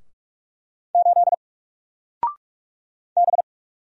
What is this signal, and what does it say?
Morse code, heard as a keyed tone, about 45 wpm, 700 Hz. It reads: 8 B
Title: Single Letter-Number 45wpm